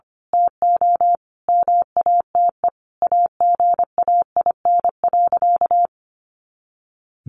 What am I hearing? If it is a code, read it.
Morse code, heard as a keyed tone, about 25 wpm, 700 Hz. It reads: TO MATE AGAIN.
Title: A Sound Of Thunder 25wpm